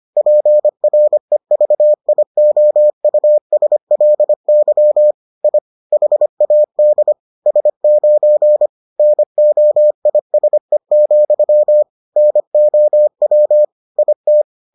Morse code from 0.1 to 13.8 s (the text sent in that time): PREVIOUSLY I HAD S9 NOISE, NOW